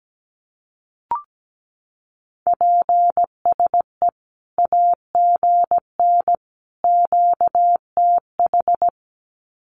Morse code, heard as a keyed tone, about 17 wpm, 700 Hz. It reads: PSE AGN QTH
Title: Common QSO Phrases 17wpm